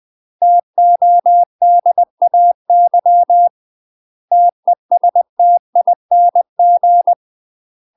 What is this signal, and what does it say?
Morse code, heard as a keyed tone, about 20 wpm, 700 Hz. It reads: TODAY TESTING
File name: UK MASTER QSOs TRAINING / 20WPM / UK MASTER QSOs 0047